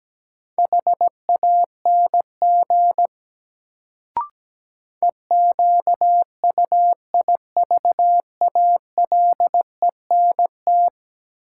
Morse code, heard as a keyed tone, about 17 wpm, 700 Hz. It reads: HANG EQUIVALENT
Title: Sentences from Top 1600 Words - New Word Drill - 17wpm